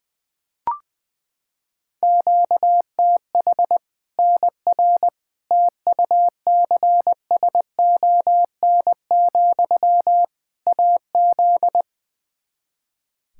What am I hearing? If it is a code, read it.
Morse code, heard as a keyed tone, about 20 wpm, 700 Hz. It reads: QTH NR TUCSON, AZ
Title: K1USN SST QSO Practice 20wpm